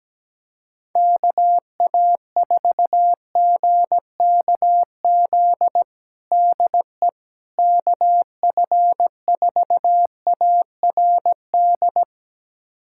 Morse code, heard as a keyed tone, about 17 wpm, 700 Hz. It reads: KA4GKZ DE KF4ARD
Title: USA Callsign DE Callsign - 6 Characters - 17wpm